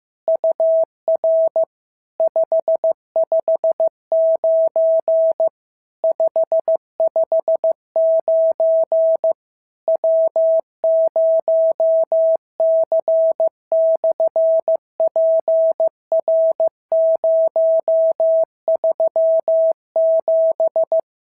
Morse code, heard as a keyed tone, about 15 wpm, 650 Hz. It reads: UR 559 559 W0C/PR037
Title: SOTA QSO Practice 15wpm